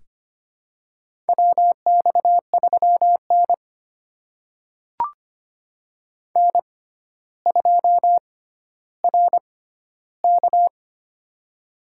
Morse code, heard as a keyed tone, about 25 wpm, spaced as for 13 wpm, 700 Hz. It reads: WX3N N2RK
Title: USA Call Signs - 4 Characters - 6x Character Spacing - 25wpm